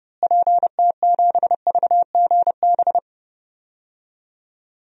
Morse code, heard as a keyed tone, about 29 wpm, 700 Hz. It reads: PT7VGB
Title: DXCC-100 Call Signs - 6 Characters - Repeated 2x - 30wpm